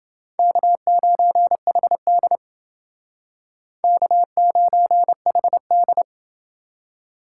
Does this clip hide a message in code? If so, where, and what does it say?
Yes: Morse code from 0.4 to 6.0 s, heard as a keyed tone, about 30 wpm, 690 Hz. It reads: K9HD K9HD